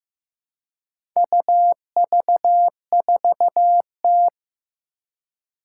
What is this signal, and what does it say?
Morse code, heard as a keyed tone, about 15 wpm, 700 Hz. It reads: UV4T
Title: Four Challenging Letter-Number Combinations 15wpm